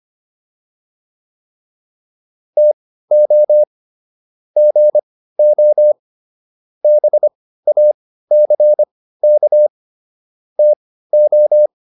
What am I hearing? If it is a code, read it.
Morse code, heard as a keyed tone, about 25 wpm, spaced as for 15 wpm, 600 Hz. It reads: TO GO BACK TO